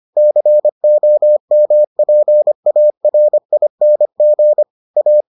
Morse code, heard as a keyed tone, about 25 wpm, 600 Hz. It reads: COMPARING A